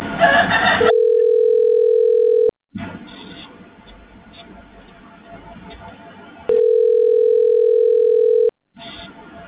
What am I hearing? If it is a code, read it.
Ringback tone (North America), 440 + 480 Hz.